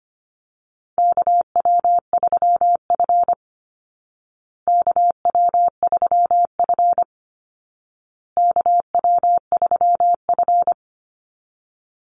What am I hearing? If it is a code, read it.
Morse code, heard as a keyed tone, about 25 wpm, 700 Hz. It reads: KW3F KW3F KW3F
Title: Call Signs Repeated 2x - 25wpm